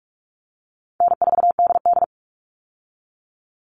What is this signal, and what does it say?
Morse code, heard as a keyed tone, about 46 wpm, 700 Hz. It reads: N4DD
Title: Easy Call Signs Repeated 3x - 45wpm